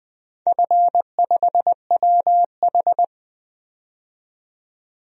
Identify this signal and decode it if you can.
Morse code, heard as a keyed tone, about 20 wpm, 700 Hz. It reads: F5WH
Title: DXCC-100 Call Signs - 4 Characters - 20wpm